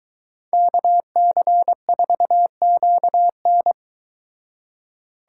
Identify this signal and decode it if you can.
Morse code, heard as a keyed tone, about 23 wpm, 700 Hz. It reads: KC4QN